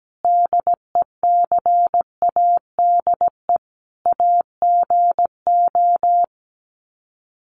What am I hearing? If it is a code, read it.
Morse code, heard as a keyed tone, about 17 wpm, 700 Hz. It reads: DECADE AGO
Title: Sets of 5 Words Encore2 17wpm